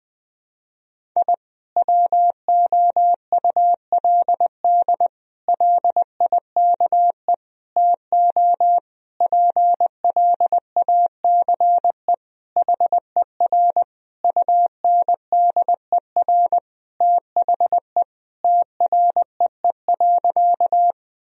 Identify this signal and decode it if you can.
Morse code, heard as a keyed tone, about 20 wpm, 700 Hz. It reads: I WOULD LIKE TO PLACE HER UNDER THE TREE.
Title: Sentences from Top 1000 Words - Review Part 1 - 20wpm